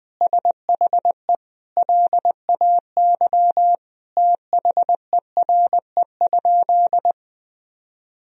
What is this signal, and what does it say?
Morse code, heard as a keyed tone, about 20 wpm, 700 Hz. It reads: SHE LAY THERE?
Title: Sentences from Top 1000 Words - Review Part 2 - 20wpm